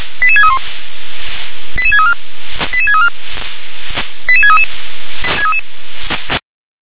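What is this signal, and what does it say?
Five-tone selective call (ZVEI), heard as quick runs of five tones, tones of about 70 ms. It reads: 88631 88634 88634 88633
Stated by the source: ZVEI five-tone selective call 88730 received on an SDR.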